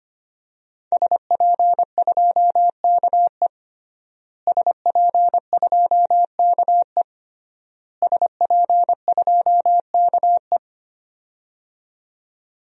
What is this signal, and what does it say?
Morse code, heard as a keyed tone, about 25 wpm, 700 Hz. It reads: SP2KE SP2KE SP2KE
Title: DXCC-100 Call Signs - 5 Characters - Repeated 3x - 25wpm